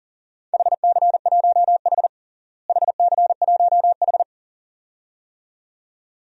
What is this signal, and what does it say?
Morse code, heard as a keyed tone, about 40 wpm, 700 Hz. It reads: HC1H HC1H
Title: DXCC-100 Call Signs - 4 Characters - Repeated 2x - 40wpm